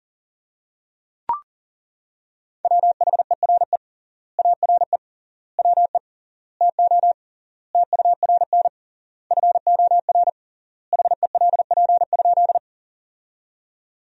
Morse code, heard as a keyed tone, about 40 wpm, 700 Hz. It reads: WHERE ARE WE TO TURN FOR HELP?
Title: 3x Word-Spacing - Sentences from Top 200 Words - 40wpm